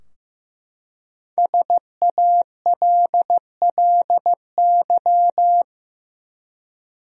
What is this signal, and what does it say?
Morse code, heard as a keyed tone, about 15 wpm, 700 Hz. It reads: SALLY
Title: USA Common Names - Max 5 Letters - 15wpm